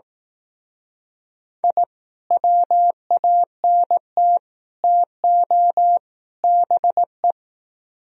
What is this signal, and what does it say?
Morse code, heard as a keyed tone, about 18 wpm, 700 Hz. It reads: I WANT TO BE
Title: Speed-Racing - Sets of 4 Words 15wpm